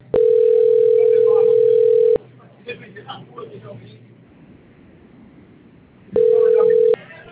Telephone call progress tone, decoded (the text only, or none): ringback tone